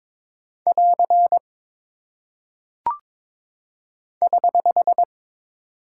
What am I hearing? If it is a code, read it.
Morse code, heard as a keyed tone, about 22 wpm, 700 Hz. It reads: <AR> <HH>
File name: practice / 22wpm / Single Punctuation-Prosign - 22wpm